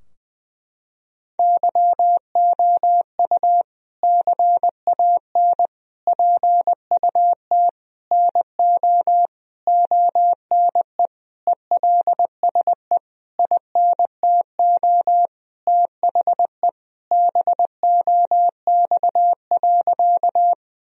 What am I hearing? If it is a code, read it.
Morse code, heard as a keyed tone, about 20 wpm, 700 Hz. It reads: YOU CAN PUT NO ONE ELSE INTO THE BOX.
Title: Sentences from Top 1000 Words - Review Part 2 - 20wpm